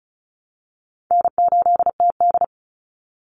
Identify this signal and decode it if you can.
Morse code, heard as a keyed tone, about 34 wpm, 700 Hz. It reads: N8TD
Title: Call Signs - 35wpm